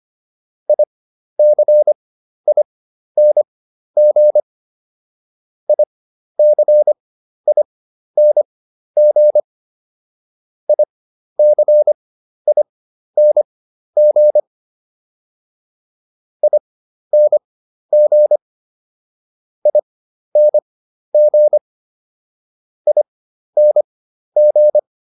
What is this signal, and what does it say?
Morse code, heard as a keyed tone, about 25 wpm, spaced as for 12 wpm, 600 Hz. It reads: ICING ICING ICING ING ING ING